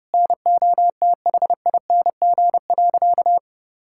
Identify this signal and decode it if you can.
Morse code, heard as a keyed tone, about 32 wpm, 700 Hz. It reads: NOTHING.
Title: Sentences from Top 1000 Words - Review Part 5 - 30wpm